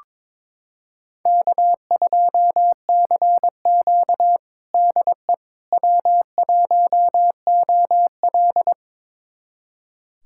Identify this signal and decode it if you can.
Morse code, heard as a keyed tone, about 21 wpm, 700 Hz. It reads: K2CQ DE W1OL
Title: USA Callsign DE Callsign - 4 Characters - 22wpm